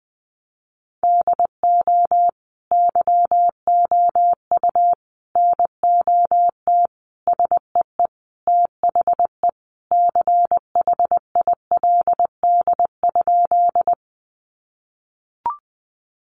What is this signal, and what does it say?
Morse code, heard as a keyed tone, about 20 wpm, 700 Hz. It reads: DO YOU NOT SEE THE CHILD?
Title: Sentences from Top 800 Words 20wpm